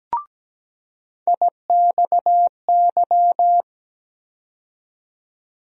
Morse code, heard as a keyed tone, about 17 wpm, 700 Hz. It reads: IXY
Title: USA Call Sign Postfixes - 3 Letters 17wpm